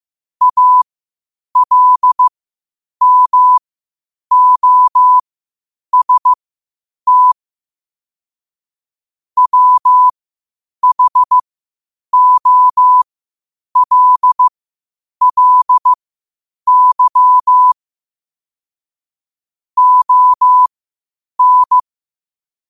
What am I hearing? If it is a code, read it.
Morse code, heard as a keyed tone, about 15 wpm, spaced as for 8 wpm, 1000 Hz. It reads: ALMOST WHOLLY ON
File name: qotdfiles / 8wpm